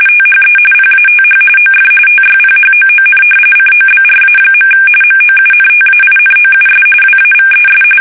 RTTY, about 45.45 baud. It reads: THE PATH OF TYPHOON COBRA. MOUNTAINOUS WAVES
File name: fsk850 / battleshipnj